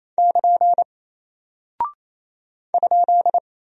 Morse code, heard as a keyed tone, about 28 wpm, 700 Hz. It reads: <KN> ?
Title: Single Character (Letter:Number:Punctuation:Prosign) Alt 28wpm